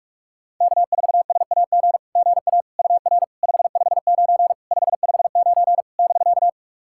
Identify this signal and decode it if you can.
Morse code, heard as a keyed tone, about 46 wpm, 690 Hz. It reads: K4SAG GA UR 559 559 <BK>
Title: SOTA QSO Practice 45wpm